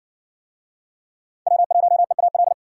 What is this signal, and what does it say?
Morse code, heard as a keyed tone, about 53 wpm, 700 Hz. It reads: W1EAL